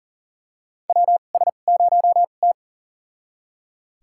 Morse code, heard as a keyed tone, about 40 wpm, spaced as for 29 wpm, 700 Hz. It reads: WS0T
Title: USA Call Signs - 2x Character Spacing - 40wpm